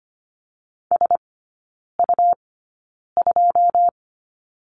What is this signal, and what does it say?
Morse code, heard as a keyed tone, about 25 wpm, spaced as for 9 wpm, 700 Hz. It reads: SU2
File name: Lesson 17e - Letter u - Three Characters 25wpm